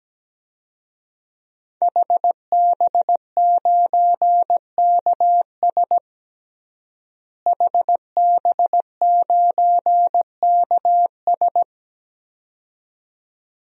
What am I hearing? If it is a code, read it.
Morse code, heard as a keyed tone, about 17 wpm, 700 Hz. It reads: HB9KS HB9KS
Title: DXCC-100 Call Signs - 5 Characters - Repeated 2x - 17wpm